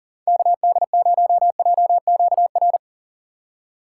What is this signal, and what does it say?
Morse code, heard as a keyed tone, about 49 wpm, 700 Hz. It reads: KD0JQR
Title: Speed Racing - USA 2x3 Call Signs 40wpm